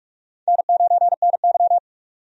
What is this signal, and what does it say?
Morse code, heard as a keyed tone, about 44 wpm, 700 Hz. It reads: N9NY
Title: USA Call Signs - 4 Characters - 45wpm